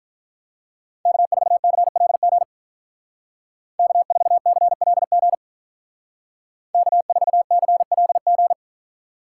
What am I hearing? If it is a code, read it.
Morse code, heard as a keyed tone, about 43 wpm, 700 Hz. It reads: K4CLG K4CLG K4CLG
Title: Speed Racing - USA Call Signs 35wpm